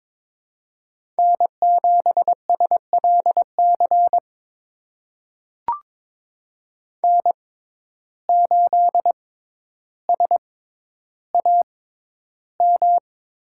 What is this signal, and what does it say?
Morse code, heard as a keyed tone, about 22 wpm, spaced as for 11 wpm, 700 Hz. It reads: N7SLC N8SAM
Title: USA Call Signs - 5 Characters - 6x Character Spacing - 22wpm